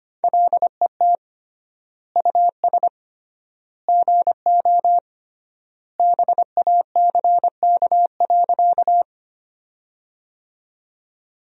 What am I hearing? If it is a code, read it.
Morse code, heard as a keyed tone, about 25 wpm, 700 Hz. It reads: LET US GO BACK.
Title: 3x Word-Spacing - Sentences from Top 300 Words - 25wpm